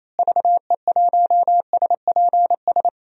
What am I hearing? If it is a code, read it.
Morse code, heard as a keyed tone, about 28 wpm, 700 Hz. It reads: VE1SPS
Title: DXCC-100 Call Signs - 6 Characters - Repeated 2x - 28wpm